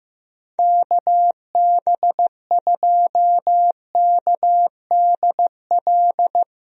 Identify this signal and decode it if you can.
Morse code, heard as a keyed tone, about 15 wpm, 700 Hz. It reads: KB2KDL
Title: USA Call Signs - 6 Characters - 15wpm